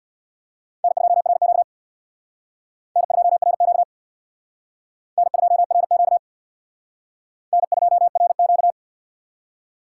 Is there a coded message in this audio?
Yes: Morse code from 0.8 to 8.7 s, heard as a keyed tone, about 67 wpm, 700 Hz. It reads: N2RQ N2RQ N2RQ N2RQ